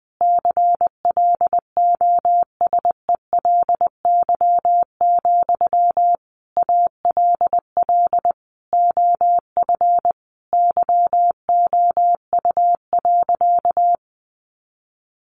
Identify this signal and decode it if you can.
Morse code, heard as a keyed tone, about 20 wpm, 700 Hz. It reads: CLOSELY, ALL OF YOU.
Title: Sentences from Top 1400 Words 20wpm